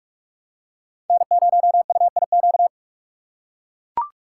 Morse code, heard as a keyed tone, about 41 wpm, 700 Hz. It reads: N0UIQ
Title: USA Call Signs - 5 Characters - 2x Character Spacing - 45wpm